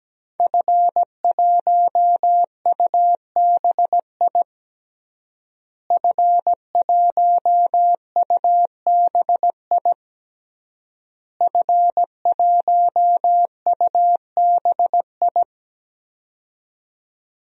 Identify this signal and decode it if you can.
Morse code, heard as a keyed tone, about 17 wpm, 700 Hz. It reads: F1UBI F1UBI F1UBI
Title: DXCC-100 Call Signs - 5 Characters - Repeated 3x - 17wpm